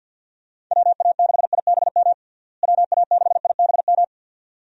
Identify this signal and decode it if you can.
Morse code, heard as a keyed tone, about 50 wpm, 700 Hz. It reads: WA6IBM WA6IBM
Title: USA Call Signs - 6 Characters - Repeated 2x - 50wpm